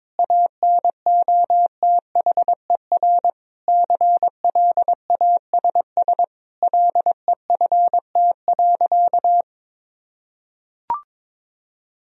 Morse code, heard as a keyed tone, about 22 wpm, 700 Hz. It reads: ANOTHER CLASS LEFT.
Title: Sentences from Top 1000 Words - Review Part 2 - 22wpm